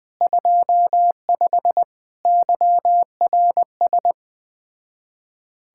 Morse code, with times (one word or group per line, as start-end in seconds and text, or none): 0.2-1.8 25
2.3-4.1 YRS